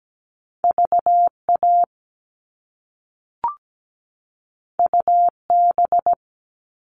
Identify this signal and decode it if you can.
Morse code, heard as a keyed tone, about 17 wpm, 700 Hz. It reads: VA UB
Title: All Cut Numbers - 2-Digit 17wpm